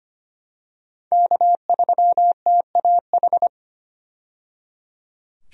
Morse code, heard as a keyed tone, about 25 wpm, 700 Hz. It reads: K3TAH
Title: USA Call Signs - 5 Characters - 25wpm